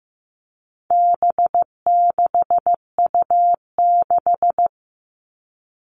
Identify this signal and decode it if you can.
Morse code, heard as a keyed tone, about 15 wpm, 700 Hz. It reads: B6U6
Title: All Cut Numbers - 4-Digit 15wpm